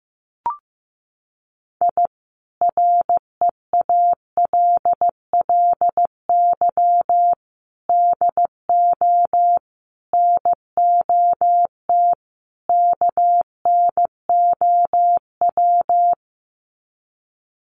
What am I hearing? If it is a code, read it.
Morse code, heard as a keyed tone, about 14 wpm, 700 Hz. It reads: I REALLY DO NOT KNOW
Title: Sets of 5 Words Encore 15wpm